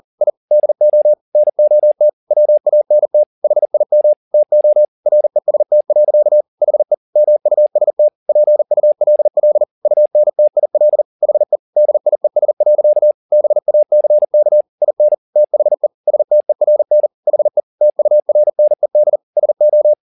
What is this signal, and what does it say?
Morse code, heard as a keyed tone, about 40 wpm, 600 Hz. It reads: I DO NOT WANT HIM TO REST. HE MUST PULL UNTIL HE DIES. BACK IN THE STERN HE TURNED SO
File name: the old man and the sea 2 40wpm